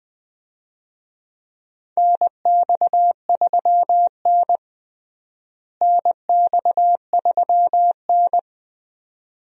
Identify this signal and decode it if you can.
Morse code, heard as a keyed tone, about 20 wpm, 700 Hz. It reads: NX3N NX3N
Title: USA Call Signs - 4 Characters - Repeated 2x - 20wpm